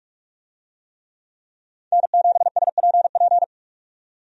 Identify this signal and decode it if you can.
Morse code, heard as a keyed tone, about 47 wpm, 700 Hz. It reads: N7SPP
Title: USA Call Signs - 5 Characters - 45wpm